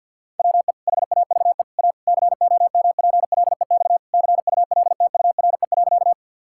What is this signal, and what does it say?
Morse code, heard as a keyed tone, about 50 wpm, 700 Hz. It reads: WE HAVE A COMPLEX CULTURE.
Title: Sentences from Top 1000 Words - Review Part 6 - 50wpm